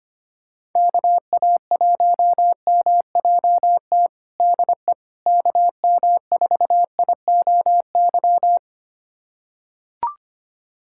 Morse code, heard as a keyed tone, about 25 wpm, 700 Hz. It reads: KA1MJT DE KM4IOY
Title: USA Callsign DE Callsign - 6 Characters - 25wpm